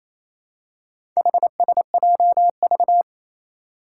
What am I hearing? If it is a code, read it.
Morse code, heard as a keyed tone, about 28 wpm, 700 Hz. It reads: HSJV
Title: Four Challenging Letter-Number Combinations 28wpm